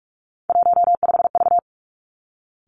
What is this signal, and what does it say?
Morse code, heard as a keyed tone, about 46 wpm, 700 Hz. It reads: 15V
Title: Three Challenging Letter-Number Combinations 45wpm